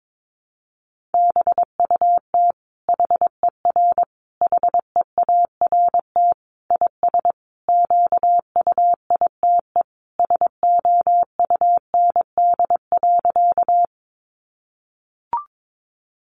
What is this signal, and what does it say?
Morse code, heard as a keyed tone, about 22 wpm, 700 Hz. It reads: BUT HER HEART IS QUITE SOUND.
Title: Sentences from Top 1200 Words 22wpm